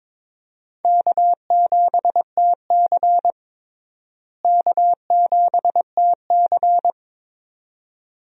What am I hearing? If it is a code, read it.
Morse code, heard as a keyed tone, about 22 wpm, 700 Hz. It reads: K7TC K7TC
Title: USA Call Signs - 4 Characters - Repeated 2x - 22wpm